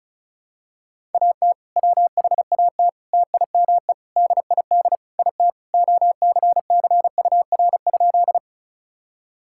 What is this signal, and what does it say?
Morse code, heard as a keyed tone, about 36 wpm, 700 Hz. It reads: AT WHAT TIME DID IT OCCUR?
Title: Sentences from Top 1000 Words - Review Part 6 - 35wpm